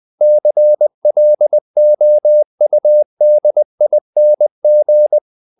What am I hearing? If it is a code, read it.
Morse code, heard as a keyed tone, about 20 wpm, 600 Hz. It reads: CLOUDING
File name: the old man and the sea part 2 20wpm